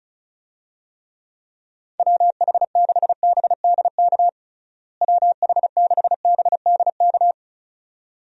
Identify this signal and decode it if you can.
Morse code, heard as a keyed tone, about 36 wpm, 700 Hz. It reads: WH6BDK WH6BDK
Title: USA Call Signs - 6 Characters - Repeated 2x - 35wpm